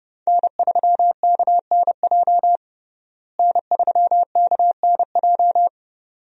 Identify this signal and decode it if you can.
Morse code, heard as a keyed tone, about 32 wpm, 700 Hz. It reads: N3KNJ N3KNJ
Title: USA Call Signs - 5 Characters - Repeated 2x - 30wpm